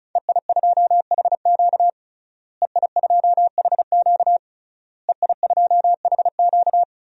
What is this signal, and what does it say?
Morse code, heard as a keyed tone, about 36 wpm, 700 Hz. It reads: EI2HQ EI2HQ EI2HQ
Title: DXCC-100 Call Signs - 5 Characters - Repeated 3x - 35wpm